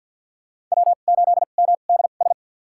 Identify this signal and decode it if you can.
Morse code, heard as a keyed tone, about 53 wpm, spaced as for 37 wpm, 700 Hz. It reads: W8MDS
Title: USA Call Signs - 5 Characters - 2x Character Spacing - 50wpm